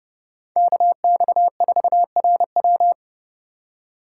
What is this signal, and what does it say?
Morse code, heard as a keyed tone, about 29 wpm, 700 Hz. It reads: KX4RW